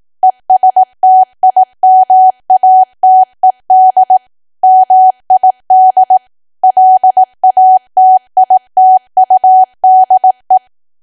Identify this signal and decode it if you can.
Morse code, heard as a keyed tone, about 18 wpm, 750 Hz. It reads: ESTIMATED MID LATITUDE
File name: cwa / W1AW-Level2 / 18WPM